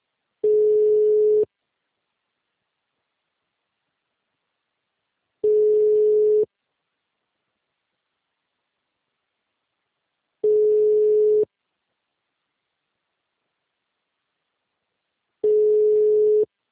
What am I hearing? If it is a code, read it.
Ringback tone (Europe, 425 Hz).